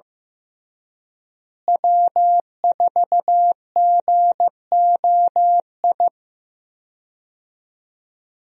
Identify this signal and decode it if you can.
Morse code, heard as a keyed tone, about 15 wpm, 700 Hz. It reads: W4GOI
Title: USA Call Signs - 5 Characters - Repeated 3x - 15wpm